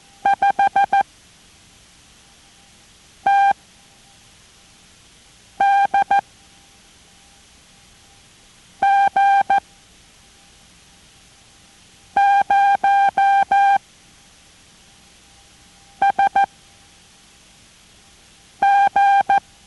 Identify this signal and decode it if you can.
Morse code, heard as a keyed tone, about 14 wpm, 790 Hz. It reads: 5 T D G 0 S G